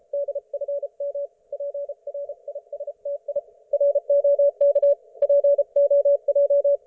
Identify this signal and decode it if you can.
Morse code, heard as a keyed tone, about 38 wpm, spaced as for 32 wpm, 570 Hz. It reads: DFM PRISTI ROK POJ
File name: Archiv / CUC224 / ok1fhi cul224